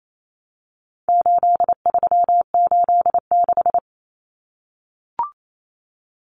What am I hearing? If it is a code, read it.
Morse code, heard as a keyed tone, about 28 wpm, 700 Hz. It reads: 8386
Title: Numbers - 4-Digit 28wpm